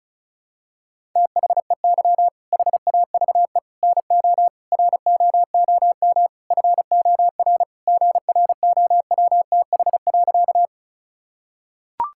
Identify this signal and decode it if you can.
Morse code, heard as a keyed tone, about 34 wpm, 690 Hz. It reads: THEY HAVE NO ROOM FOR GROWTH.
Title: Sentences from Top 1000 Words - Review Part 5 - 35wpm